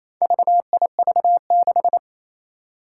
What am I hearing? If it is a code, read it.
Morse code, heard as a keyed tone, about 28 wpm, 700 Hz. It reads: VIV6
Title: Four Challenging Letter-Number Combinations 28wpm